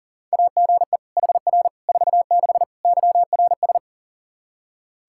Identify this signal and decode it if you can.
Morse code, heard as a keyed tone, about 40 wpm, 700 Hz. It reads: AGE HR 46 YRS
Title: Practice QSO - Exchange 3 - More Info - 40wpm